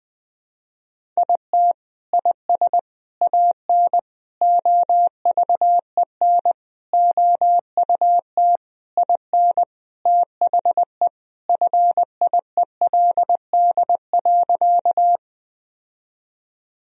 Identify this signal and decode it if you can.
Morse code, heard as a keyed tone, about 20 wpm, 700 Hz. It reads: IT IS AN OVEN OUT IN THE FIELD.
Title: Sentences from Top 1000 Words - Review Part 4 - 20wpm